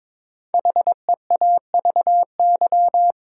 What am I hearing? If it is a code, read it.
Morse code, heard as a keyed tone, about 22 wpm, 700 Hz. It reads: HEAVY
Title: Sentences from Top 1100 Words - New Word Drill Alt - 22wpm